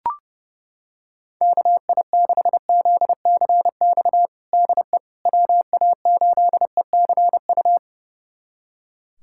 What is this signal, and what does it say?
Morse code, heard as a keyed tone, about 30 wpm, 700 Hz. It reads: KI6ZCX DE WA8ECU
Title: USA Callsign DE Callsign - 6 Characters - 30wpm